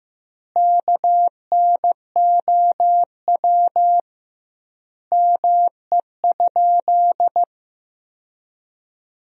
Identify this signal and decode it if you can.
Morse code, heard as a keyed tone, about 15 wpm, 700 Hz. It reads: KNOW ME?
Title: 2x Word-Spacing - Sentences from Top 200 Words - 15wpm